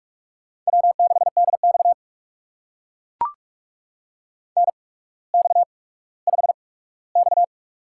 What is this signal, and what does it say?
Morse code, heard as a keyed tone, about 44 wpm, spaced as for 19 wpm, 700 Hz. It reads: W6DX NX5X
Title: USA Call Signs - 4 Characters - 8x Character Spacing - 45wpm